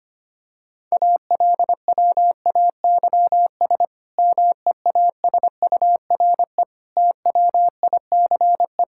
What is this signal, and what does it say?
Morse code, heard as a keyed tone, about 25 wpm, 700 Hz. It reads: ALWAYS MEASURE TWICE
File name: practice / 25wpm / Difficult Sentences with Hints - 25wpm